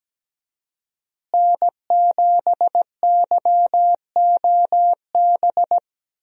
Morse code, heard as a keyed tone, about 17 wpm, 700 Hz. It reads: N7YOB
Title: USA Call Signs - 5 Characters - Repeated 2x - 17wpm